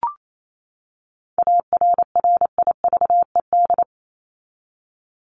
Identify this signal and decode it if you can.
Morse code, heard as a keyed tone, about 27 wpm, 700 Hz. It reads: ARRIVED
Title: ED Suffix - Top 100 Words with Base Words 28wpm